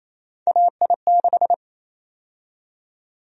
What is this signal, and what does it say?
Morse code, heard as a keyed tone, about 28 wpm, 700 Hz. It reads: AI6
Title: USA Call Sign Prefixes - 28wpm